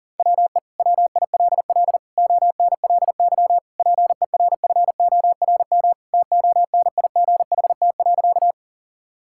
Morse code, heard as a keyed tone, about 37 wpm, 700 Hz. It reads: WE WILL ONLY PERFORM TONIGHT.
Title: Sentences from Top 1000 Words - Review Part 5 - 40wpm